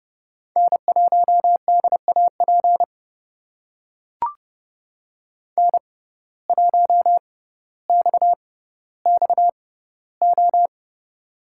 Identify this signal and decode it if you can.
Morse code, heard as a keyed tone, about 32 wpm, spaced as for 16 wpm, 700 Hz. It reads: N1DAP N1XXO
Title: USA Call Signs - 5 Characters - 6x Character Spacing - 30wpm